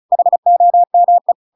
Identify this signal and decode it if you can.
Morse code, heard as a keyed tone, about 36 wpm, 700 Hz. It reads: HOME
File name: Pinocchio / Pinocchio ch8 35 35 ik1hge